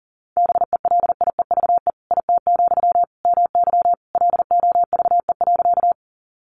Morse code, heard as a keyed tone, about 40 wpm, 700 Hz. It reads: BELIEVE IT, MY LOVE.
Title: Sentences from Top 1000 Words 40wpm